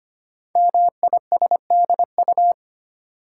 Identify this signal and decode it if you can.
Morse code, heard as a keyed tone, about 25 wpm, 700 Hz. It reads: MISDU
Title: Code Groups - 5 Letters - 25wpm